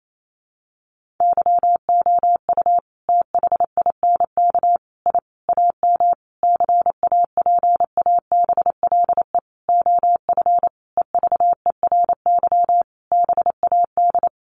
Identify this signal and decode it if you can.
Morse code, heard as a keyed tone, about 28 wpm, 700 Hz. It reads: YOU THINK I AM CAPABLE OF EVERY BAD
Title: Sentences from Top 1200 Words 28wpm